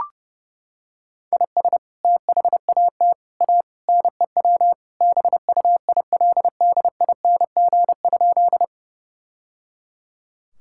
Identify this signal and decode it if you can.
Morse code, heard as a keyed tone, about 30 wpm, 700 Hz. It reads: IS THAT A NEW BUILDING?
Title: Sentences from Top 1000 Words - Review Part 5 - 30wpm